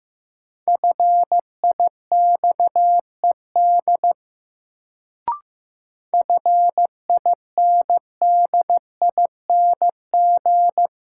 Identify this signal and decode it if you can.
Morse code, heard as a keyed tone, about 15 wpm, 700 Hz. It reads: FIXED FINDING
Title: Sentences from Top 1100 Words - New Word Drill - 15wpm